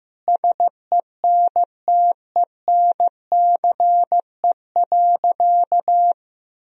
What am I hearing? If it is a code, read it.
Morse code, heard as a keyed tone, about 15 wpm, 700 Hz. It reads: SENTENCE.
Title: Sentences from Top 1000 Words - Review Part 1 - 15wpm